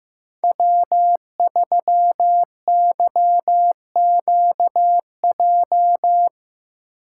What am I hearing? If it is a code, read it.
Morse code, heard as a keyed tone, about 15 wpm, 700 Hz. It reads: W3YQJ
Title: Code Groups - 5 Letter:Numbers - 15wpm